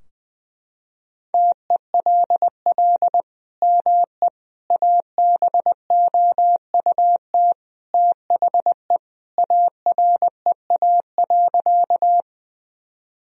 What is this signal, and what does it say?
Morse code, heard as a keyed tone, about 20 wpm, 700 Hz. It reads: TELL ME ABOUT THE AREA.
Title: Sentences from Top 1000 Words - Review Part 2 - 20wpm